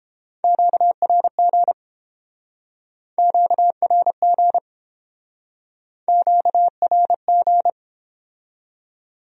Morse code, heard as a keyed tone, about 30 wpm, 700 Hz. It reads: QRG QRG QRG